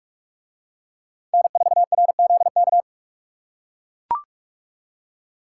Morse code, heard as a keyed tone, about 44 wpm, 700 Hz. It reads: N4RZK